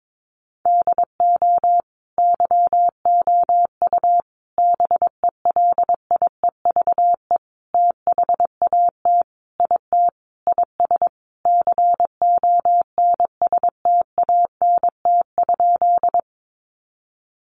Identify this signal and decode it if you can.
Morse code, heard as a keyed tone, about 22 wpm, 700 Hz. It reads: DO YOU BELIEVE THAT IT IS CONSTANT?
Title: Sentences from Top 1500 Words 22wpm